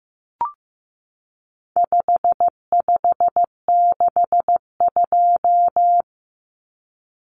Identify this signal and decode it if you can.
Morse code, heard as a keyed tone, about 15 wpm, 700 Hz. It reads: 5562
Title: Common Cut Numbers - 4-Digit 15wpm